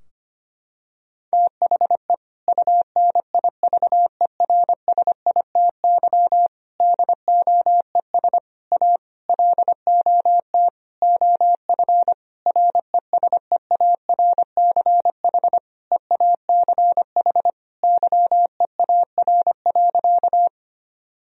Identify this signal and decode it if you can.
Morse code, heard as a keyed tone, about 25 wpm, 700 Hz. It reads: THE UNIVERSITY DOES A LOT OF RESEARCH EACH YEAR.
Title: Sentences from Top 1000 Words - Review Part 4 - 25wpm